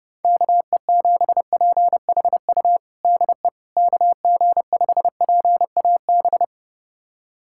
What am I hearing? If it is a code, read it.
Morse code, heard as a keyed tone, about 30 wpm, 700 Hz. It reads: KE7PHU DE KG5PAB
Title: USA Callsign DE Callsign - 6 Characters - 30wpm